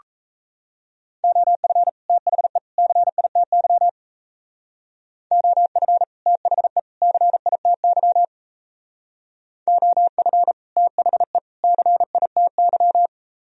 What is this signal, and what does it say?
Morse code, heard as a keyed tone, about 33 wpm, 700 Hz. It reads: OF THE CITY OF THE CITY OF THE CITY
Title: Speed-Racing - Sets of 3 Words 28wpm